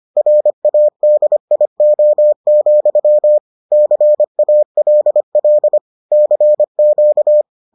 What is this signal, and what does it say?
Morse code, heard as a keyed tone, about 25 wpm, 600 Hz. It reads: RADIO, CALL CQ